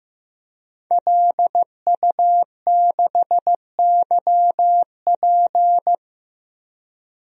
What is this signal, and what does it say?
Morse code, heard as a keyed tone, about 16 wpm, 700 Hz. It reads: LU6YP
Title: DXCC-100 Call Signs - 5 Characters - 15wpm